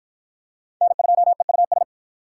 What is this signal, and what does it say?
Morse code, heard as a keyed tone, about 53 wpm, 700 Hz. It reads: N2EUS